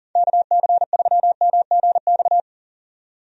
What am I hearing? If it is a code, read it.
Morse code, heard as a keyed tone, about 39 wpm, 700 Hz. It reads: KC3MGX
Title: USA Call Signs - 6 Characters - 40wpm